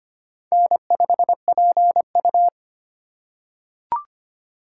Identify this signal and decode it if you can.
Morse code, heard as a keyed tone, about 25 wpm, 700 Hz. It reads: N5PU
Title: K1USN SST QSO Practice - 25wpm